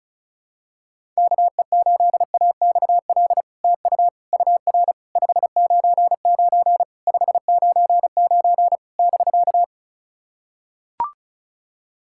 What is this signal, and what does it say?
Morse code, heard as a keyed tone, about 36 wpm, 700 Hz. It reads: KE8AXL TU UR 599 599 <BK>
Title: SOTA QSO Practice 35wpm